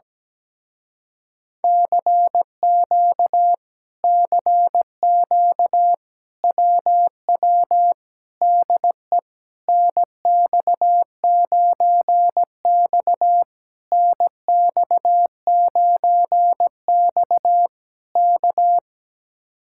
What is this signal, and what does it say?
Morse code, heard as a keyed tone, about 17 wpm, 700 Hz. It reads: CQ CQ WW DE NX9X NX9X K
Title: Speed-Racing Common QSO Phrases - 17wpm